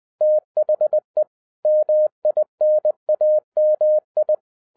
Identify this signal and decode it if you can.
Morse code, heard as a keyed tone, about 20 wpm, 610 Hz. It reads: THE MINAMI
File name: 2011-10-mp3 / n03-TARO JR0QWW 20wpm